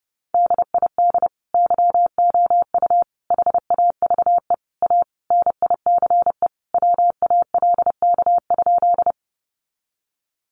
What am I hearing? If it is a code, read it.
Morse code, heard as a keyed tone, about 30 wpm, 700 Hz. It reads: DID YOU HAVE A NICE WALK?
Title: Sentences from Top 1100 Words 30wpm